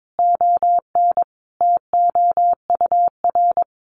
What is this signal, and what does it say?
Morse code, heard as a keyed tone, about 22 wpm, 700 Hz. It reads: ON TOUR
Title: Sentences from Top 1400 Words 22wpm